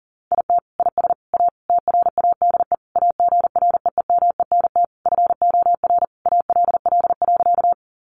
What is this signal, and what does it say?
Morse code, heard as a keyed tone, about 40 wpm, 700 Hz. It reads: IT IS A TRADE AGREEMENT FOR ALL.
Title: Sentences from Top 1500 Words 40wpm